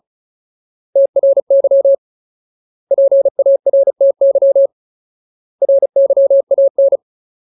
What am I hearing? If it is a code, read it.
Morse code, heard as a keyed tone, about 35 wpm, 550 Hz. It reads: TRY PARTY RYAN
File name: Ultimate Word List / 35WPM Ultimate